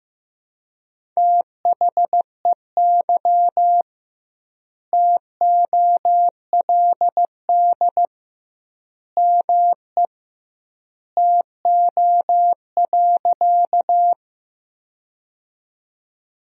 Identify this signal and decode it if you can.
Morse code, heard as a keyed tone, about 16 wpm, 700 Hz. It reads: THEY TOLD ME TO.
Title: 2x Word-Spacing - Sentences from Top 400 Words - 15wpm